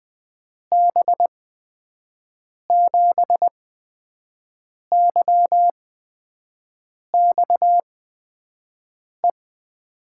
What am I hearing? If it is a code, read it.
Morse code, heard as a keyed tone, about 20 wpm, spaced as for 5 wpm, 700 Hz. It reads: B7YXE
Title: USA Call Signs - 8x Character Spacing - 20wpm